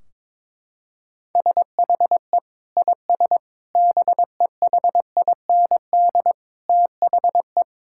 Morse code, heard as a keyed tone, about 22 wpm, 700 Hz. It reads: SHE IS BEHIND THE
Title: Sentences from Top 1600 Words 22wpm